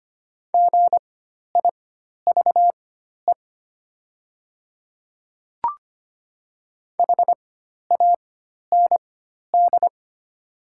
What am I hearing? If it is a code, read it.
Morse code, heard as a keyed tone, about 25 wpm, spaced as for 12 wpm, 700 Hz. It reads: GIVE HAND
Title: 4x Character Spacing - Top 200 Words - 25wpm